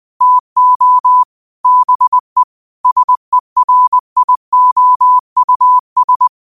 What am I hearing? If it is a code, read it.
Morse code, heard as a keyed tone, about 20 wpm, 1000 Hz. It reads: TO BE SERIOUS
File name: 20wpm